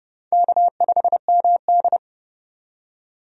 Morse code, heard as a keyed tone, about 30 wpm, 700 Hz. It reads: K5MD
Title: USA Call Signs - 4 Characters - 30wpm